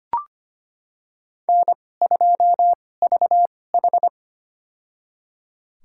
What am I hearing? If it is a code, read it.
Morse code, heard as a keyed tone, about 25 wpm, spaced as for 18 wpm, 700 Hz. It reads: N2VH
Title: USA Call Signs - 2x Character Spacing - 25wpm